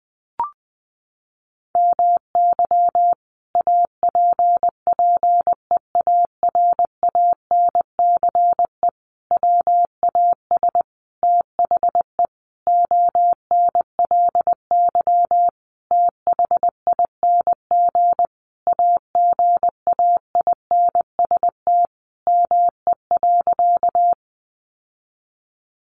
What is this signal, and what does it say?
Morse code, heard as a keyed tone, about 20 wpm, 700 Hz. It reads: MY APPEARANCE WAS THE ONLY THING AGAINST ME.
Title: Sentences from Top 1100 Words 20wpm